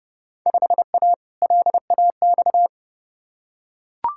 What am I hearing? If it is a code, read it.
Morse code, heard as a keyed tone, about 30 wpm, 700 Hz. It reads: 5A LAX
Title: Field Day Exchanges - 30wpm